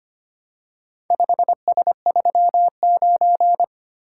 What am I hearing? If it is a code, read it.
Morse code, heard as a keyed tone, about 25 wpm, 700 Hz. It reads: 5S39
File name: practice / 25wpm / Four Letter-Number Combinations - 25wpm